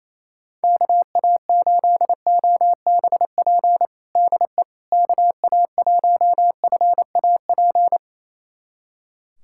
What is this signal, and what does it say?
Morse code, heard as a keyed tone, about 27 wpm, 700 Hz. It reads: KA8OBP DE KA1FAP
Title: USA Callsign DE Callsign - 6 Characters - 28wpm